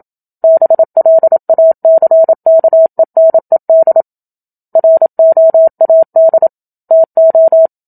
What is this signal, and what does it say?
Morse code, heard as a keyed tone, about 27 wpm, 650 Hz. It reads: BLACKENED ROAD TO
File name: Book Two / 01-Chapter 1 - 27 WPM